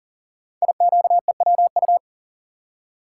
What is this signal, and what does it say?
Morse code, heard as a keyed tone, about 39 wpm, 700 Hz. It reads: IQEWU
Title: Code Groups - 5 Letters - 40wpm